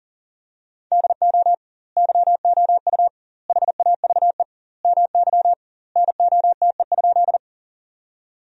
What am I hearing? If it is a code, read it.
Morse code, heard as a keyed tone, about 41 wpm, 700 Hz. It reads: DO YOU HAVE MY NOTE?
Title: 2x Word-Spacing - Sentences from Top 500 Words - 40wpm